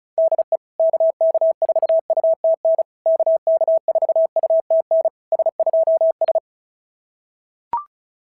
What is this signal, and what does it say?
Morse code, heard as a keyed tone, about 33 wpm, 660 Hz. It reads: DE KK4UTN KK4UTN S2S
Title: SOTA QSO Practice 35wpm